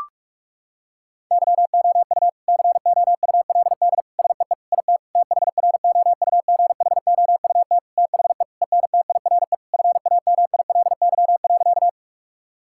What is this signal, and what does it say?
Morse code, heard as a keyed tone, about 46 wpm, 700 Hz. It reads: YOU COULD SEE IT THROUGHOUT THE ENTIRE FAMILY.
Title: Sentences from Top 1000 Words - Review Part 5 - 45wpm